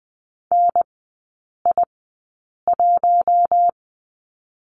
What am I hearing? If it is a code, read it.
Morse code, heard as a keyed tone, about 20 wpm, spaced as for 8 wpm, 700 Hz. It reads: NI1
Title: 047 Lesson 9e - Number 4 - Three Characters 20wpm